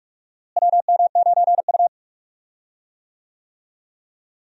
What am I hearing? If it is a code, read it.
Morse code, heard as a keyed tone, about 46 wpm, 700 Hz. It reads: WM9U
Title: USA Call Signs - 4 Characters - Repeated 2x - 45wpm